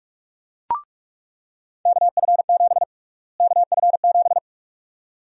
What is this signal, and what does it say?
Morse code, heard as a keyed tone, about 42 wpm, 700 Hz. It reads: KF7 KF7
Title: USA Call Sign Prefixes - Repeated 2x - 45wpm